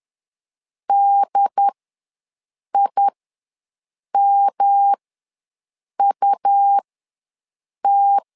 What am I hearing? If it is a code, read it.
Morse code, heard as a keyed tone, about 10 wpm, spaced as for 6 wpm, 800 Hz. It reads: DIMUT
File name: LEZIONI / 3 MWRO / 3 M ws 10wpm 2121